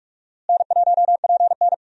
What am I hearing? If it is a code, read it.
Morse code, heard as a keyed tone, about 44 wpm, 700 Hz. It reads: N1PN
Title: USA Call Signs - 4 Characters - Repeated 2x - 45wpm